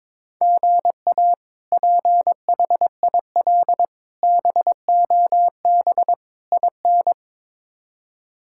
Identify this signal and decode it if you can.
Morse code, heard as a keyed tone, about 22 wpm, 700 Hz. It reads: GA PHIL BOB IN
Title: K1USN SST QSO Practice - 22wpm